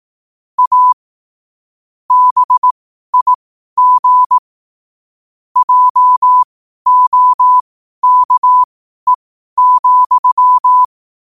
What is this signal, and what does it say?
Morse code, heard as a keyed tone, about 18 wpm, spaced as for 12 wpm, 1000 Hz. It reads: A BIG JOKE,
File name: qotdfiles / 12wpm